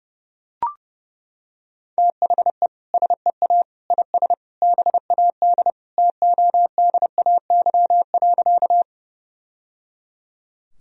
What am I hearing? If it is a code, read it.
Morse code, heard as a keyed tone, about 30 wpm, 700 Hz. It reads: THE SEA IS BAD TODAY.
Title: Sentences from Top 1000 Words - Review Part 1 - 30wpm